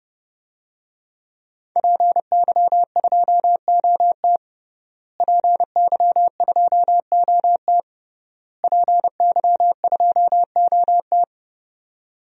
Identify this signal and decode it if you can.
Morse code, heard as a keyed tone, about 30 wpm, 700 Hz. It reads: PY2OT PY2OT PY2OT
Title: DXCC-100 Call Signs - 5 Characters - Repeated 3x - 30wpm